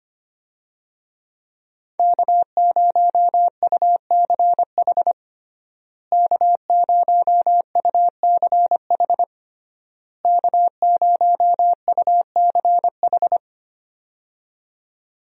Morse code, heard as a keyed tone, about 25 wpm, 700 Hz. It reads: K0UCH K0UCH K0UCH
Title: USA Call Signs - 5 Characters - Repeated 3x - 25wpm